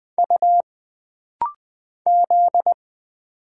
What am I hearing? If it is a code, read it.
Morse code, heard as a keyed tone, about 20 wpm, 700 Hz. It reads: U Z
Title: Single Letters - Rapid-Fire 20wpm